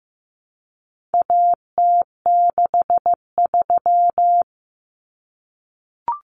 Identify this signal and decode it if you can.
Morse code, heard as a keyed tone, about 15 wpm, 700 Hz. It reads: AT63
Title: Common Cut Numbers - 4-Digit 15wpm